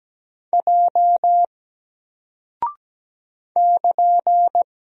Morse code, heard as a keyed tone, about 17 wpm, 700 Hz. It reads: J <KN>
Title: Single Character (Letter:Number:Punctuation:Prosign) 17wpm